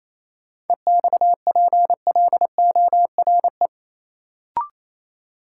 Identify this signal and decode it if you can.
Morse code, heard as a keyed tone, about 28 wpm, 700 Hz. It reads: EXPLORE
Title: Sentences from Top 1100 Words - New Word Drill Alt - 28wpm